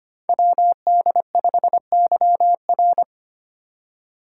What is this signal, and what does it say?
Morse code, heard as a keyed tone, about 25 wpm, 700 Hz. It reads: WD5YR
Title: Code Groups - 5 Letter:Numbers - 25wpm